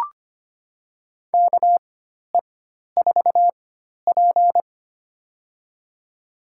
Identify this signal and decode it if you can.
Morse code, heard as a keyed tone, about 25 wpm, spaced as for 12 wpm, 700 Hz. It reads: KE4P
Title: USA Call Signs - 4 Characters - 4x Character Spacing - 25wpm